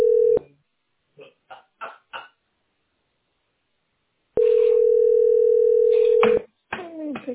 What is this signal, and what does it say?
Ringback tone (North America), 440 + 480 Hz.